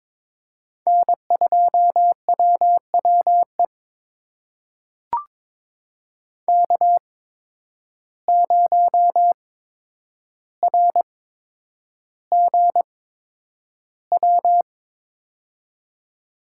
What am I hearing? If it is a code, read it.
Morse code, heard as a keyed tone, about 22 wpm, spaced as for 9 wpm, 700 Hz. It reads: N2WWE K0RGW
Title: USA Call Signs - 5 Characters - 8x Character Spacing - 22wpm